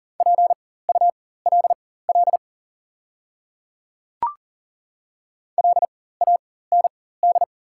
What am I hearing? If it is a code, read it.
Morse code, heard as a keyed tone, about 39 wpm, spaced as for 19 wpm, 690 Hz. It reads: PULL LAND
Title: 4x Character Spacing - Top 500 Words - 40wpm